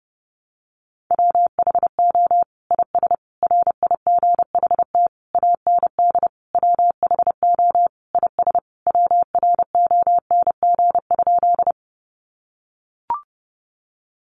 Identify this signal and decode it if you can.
Morse code, heard as a keyed tone, about 30 wpm, 700 Hz. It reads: WHO IS RIGHT AND WHO IS WRONG?
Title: Sentences from Top 900 Words 30wpm